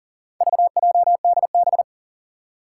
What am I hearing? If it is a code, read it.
Morse code, heard as a keyed tone, about 40 wpm, 700 Hz. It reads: VJDB
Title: Four Challenging Letter-Number Combinations 40wpm